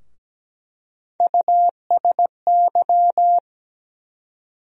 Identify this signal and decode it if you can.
Morse code, heard as a keyed tone, about 17 wpm, 700 Hz. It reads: USY
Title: USA Call Sign Postfixes - 3 Letters 17wpm